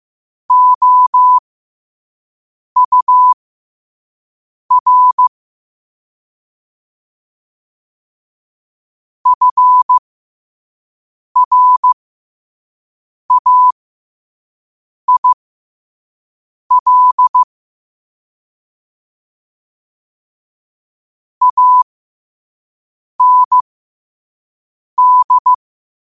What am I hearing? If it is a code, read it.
Morse code, heard as a keyed tone, about 15 wpm, spaced as for 5 wpm, 1000 Hz. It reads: OUR FRAIL AND
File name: qotdfiles / 5wpm